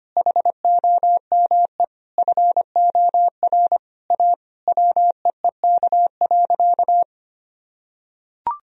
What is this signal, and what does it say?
Morse code, heard as a keyed tone, about 25 wpm, 700 Hz. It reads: HOME FOR A WEEK.
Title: Sentences from Top 1000 Words - Review Part 2 - 25wpm